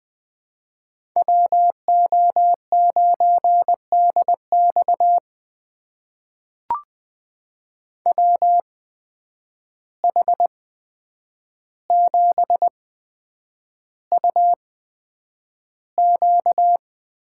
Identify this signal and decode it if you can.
Morse code, heard as a keyed tone, about 20 wpm, spaced as for 9 wpm, 700 Hz. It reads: WO9DX WH7UQ
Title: USA Call Signs - 5 Characters - 8x Character Spacing - 20wpm